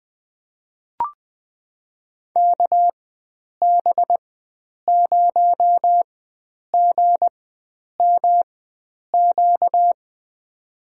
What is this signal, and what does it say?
Morse code, heard as a keyed tone, about 20 wpm, spaced as for 9 wpm, 700 Hz. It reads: KB0GMQ
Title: USA Call Signs - 6 Characters - 4x Character Spacing - 20wpm